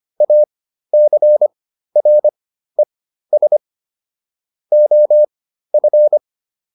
Morse code, heard as a keyed tone, about 25 wpm, spaced as for 13 wpm, 600 Hz. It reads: ACRES OF
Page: intermediate-practice-files